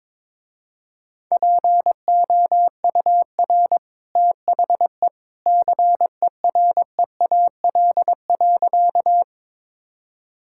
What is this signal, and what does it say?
Morse code, heard as a keyed tone, about 21 wpm, 700 Hz. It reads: POUR THE CEREAL.
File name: practice / 22wpm / Easy Sentences with Hints - 22wpm